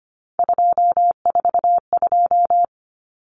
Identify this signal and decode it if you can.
Morse code, heard as a keyed tone, about 25 wpm, 700 Hz. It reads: 242
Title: Numbers - 3-Digit 25wpm